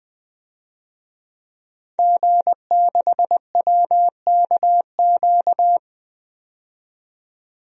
Morse code, heard as a keyed tone, about 20 wpm, 700 Hz. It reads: G6WKQ
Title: DXCC-100 Call Signs - 5 Characters - 20wpm